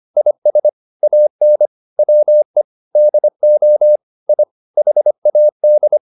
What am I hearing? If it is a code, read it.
Morse code, heard as a keyed tone, about 25 wpm, 600 Hz. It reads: IS AN WE DO I HAD